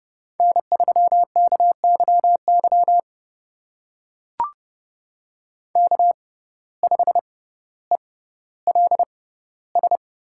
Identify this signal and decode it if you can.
Morse code, heard as a keyed tone, about 30 wpm, spaced as for 15 wpm, 700 Hz. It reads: N3KYY K5ELS
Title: USA Call Signs - 5 Characters - 6x Character Spacing - 30wpm